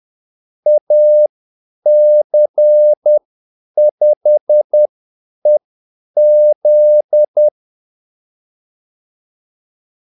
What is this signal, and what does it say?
Morse code, heard as a keyed tone, about 10 wpm, spaced as for 8 wpm, 600 Hz. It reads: AC5EZ
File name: Callsigns-10-8wpm